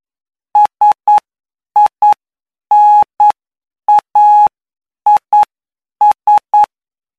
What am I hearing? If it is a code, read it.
Morse code, heard as a keyed tone, about 9 wpm, spaced as for 7 wpm, 840 Hz. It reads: SINAIS
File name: cw / py2lcd / 20-PY2LCD-Li��o 29